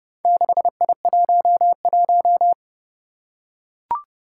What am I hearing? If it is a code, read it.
Morse code, heard as a keyed tone, about 32 wpm, 700 Hz. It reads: 6I11
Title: Four Challenging Letter-Number Combinations 30wpm